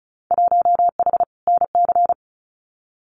Morse code, heard as a keyed tone, about 34 wpm, 700 Hz. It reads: 1H NC
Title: Winter Field Day Exchanges 35wpm